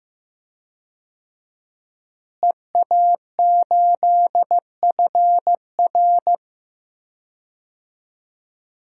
Morse code, heard as a keyed tone, about 15 wpm, 700 Hz. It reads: EA8FR
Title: DXCC-100 Call Signs - 5 Characters - Repeated 3x - 15wpm